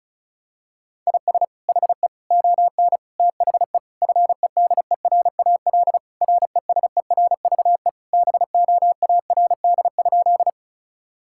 Morse code, heard as a keyed tone, about 36 wpm, 700 Hz. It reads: IS HE ON THE FEDERAL RESERVE BOARD?
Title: Sentences from Top 1000 Words - Review Part 6 - 35wpm